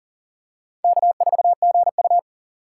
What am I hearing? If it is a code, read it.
Morse code, heard as a keyed tone, about 37 wpm, 700 Hz. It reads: K4GU
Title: USA Call Signs - 4x Character Spacing - 40wpm